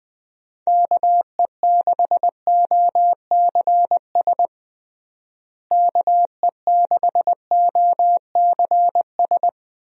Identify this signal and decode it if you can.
Morse code, heard as a keyed tone, about 21 wpm, 700 Hz. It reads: KE6OCS KE6OCS
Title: USA Call Signs - 6 Characters - Repeated 3x - 20wpm